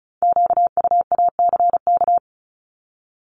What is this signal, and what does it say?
Morse code, heard as a keyed tone, about 36 wpm, 700 Hz. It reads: QUACK
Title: Multi-letter Phonemes 35wpm